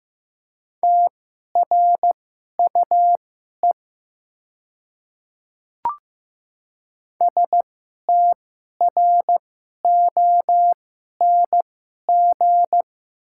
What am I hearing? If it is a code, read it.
Morse code, heard as a keyed tone, about 15 wpm, spaced as for 11 wpm, 690 Hz. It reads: TRUE STRONG
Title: 2x Character Spacing - Top 500 Words - 15wpm